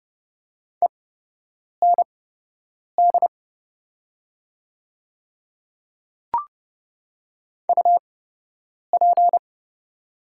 Morse code, heard as a keyed tone, about 30 wpm, spaced as for 8 wpm, 700 Hz. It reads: END UP
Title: 8x Character Spacing - Top 200 Words - 30wpm